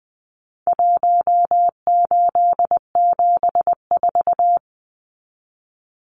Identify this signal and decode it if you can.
Morse code, heard as a keyed tone, about 20 wpm, 700 Hz. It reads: 1874
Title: Numbers - Contest Style - 4-Digit 20wpm